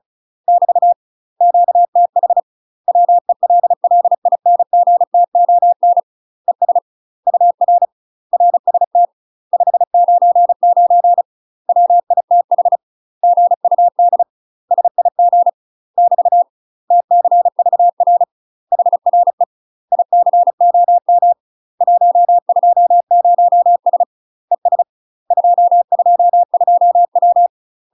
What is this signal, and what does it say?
Morse code, heard as a keyed tone, about 36 wpm, 700 Hz. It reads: <BT> QTH WELLINGTON ES UR RST 599 WITH GUD SIG <BT> TCVR HRE ICOM 120S ES 222W
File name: UK MASTER QSOs TRAINING / 35WPM / UK MASTER QSOs 0017